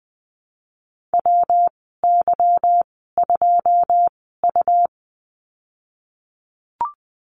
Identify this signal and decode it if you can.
Morse code, heard as a keyed tone, about 19 wpm, spaced as for 14 wpm, 700 Hz. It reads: WY2U
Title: 186 Lesson 30g - Letter b - Call Signs 20wpm